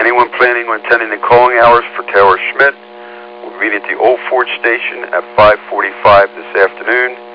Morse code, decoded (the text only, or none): none